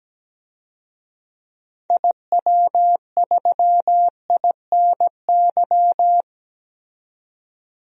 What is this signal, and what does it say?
Morse code, heard as a keyed tone, about 17 wpm, 700 Hz. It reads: IW3INY
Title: DXCC-100 Call Signs - 6 Characters - 17wpm